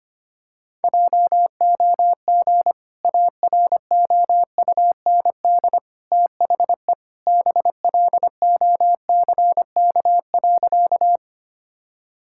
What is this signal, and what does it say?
Morse code, heard as a keyed tone, about 25 wpm, 700 Hz. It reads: JOG AROUND THE BLOCK.
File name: Easy Sentences with Hints - 25wpm